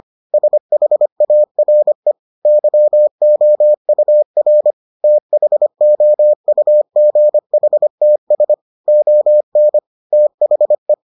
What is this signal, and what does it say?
Morse code, heard as a keyed tone, about 25 wpm, 600 Hz. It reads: SHARE YOUR THOUGHTS ON THE